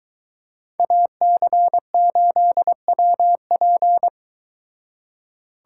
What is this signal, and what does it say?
Morse code, heard as a keyed tone, about 22 wpm, 700 Hz. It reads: AC8WP